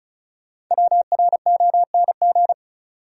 Morse code, heard as a keyed tone, about 36 wpm, 700 Hz. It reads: WRONG
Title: Sets of 5 Words Encore - New Word Drill - 35wpm